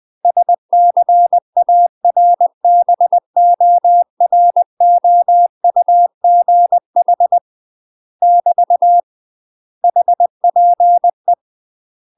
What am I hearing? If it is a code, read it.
Morse code, heard as a keyed tone, about 20 wpm, 700 Hz. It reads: SCARBOROUGH <BT> HPE
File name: UK MASTER QSOs TRAINING / 20WPM / UK MASTER QSOs 0007